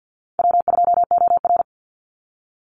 Morse code, heard as a keyed tone, about 52 wpm, 700 Hz. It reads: W2OR
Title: Easy Call Signs Repeated 3x - 50wpm